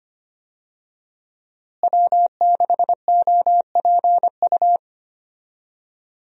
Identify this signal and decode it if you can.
Morse code, heard as a keyed tone, about 25 wpm, 700 Hz. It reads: W6OPU
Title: USA Call Signs - 5 Characters - Repeated 2x - 25wpm